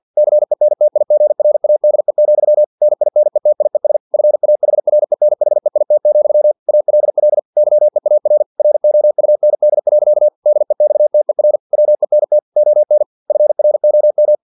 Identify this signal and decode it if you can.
Morse code, heard as a keyed tone, about 50 wpm, 600 Hz. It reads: CENTIGRADE, NINETIES FAHRENHEIT, ALL YEAR ROUND. DEXTER WENT ON FROM
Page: Never 49wpm